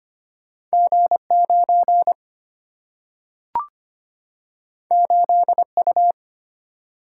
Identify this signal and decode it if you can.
Morse code, heard as a keyed tone, about 25 wpm, 700 Hz. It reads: G9 8U
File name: Two Letter-Number Combinations - 25wpm